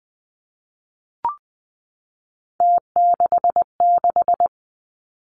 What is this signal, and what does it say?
Morse code, heard as a keyed tone, about 20 wpm, 700 Hz. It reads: T66
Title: Three Challenging Letter-Number Combinations 20wpm